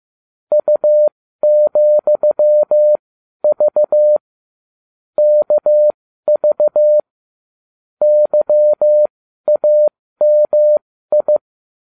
Morse code, heard as a keyed tone, about 15 wpm, spaced as for 12 wpm, 600 Hz. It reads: U,V KV YAMI
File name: cw / ubacw20 2